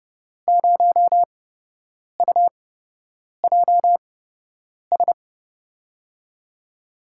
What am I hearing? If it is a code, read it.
Morse code, heard as a keyed tone, about 30 wpm, spaced as for 8 wpm, 700 Hz. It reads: 0UJS
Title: USA Call Signs - 6 Characters - 8x Character Spacing - 30wpm